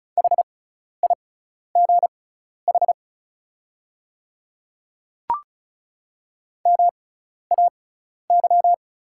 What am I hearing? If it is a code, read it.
Morse code, heard as a keyed tone, about 36 wpm, spaced as for 12 wpm, 690 Hz. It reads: HIGH MAY
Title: 6x Character Spacing - Top 200 Words - 35wpm